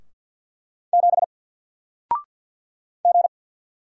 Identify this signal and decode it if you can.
Morse code, heard as a keyed tone, about 50 wpm, 700 Hz. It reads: 7 G
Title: Single Letter-Number 50wpm